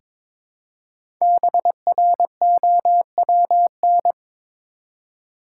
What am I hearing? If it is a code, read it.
Morse code, heard as a keyed tone, about 22 wpm, 700 Hz. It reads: BROWN
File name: 5 Letter Words - First 3 Letters Given - 22wpm